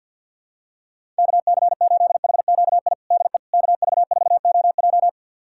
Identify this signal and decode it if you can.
Morse code, heard as a keyed tone, about 49 wpm, 700 Hz. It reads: KC8HQI DE KV4OJ
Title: USA Callsign DE Callsign 50wpm